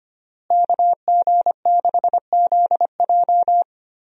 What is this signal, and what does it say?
Morse code, heard as a keyed tone, about 25 wpm, 700 Hz. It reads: KG6ZJ
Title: USA Call Signs - 5 Characters - Repeated 3x - 25wpm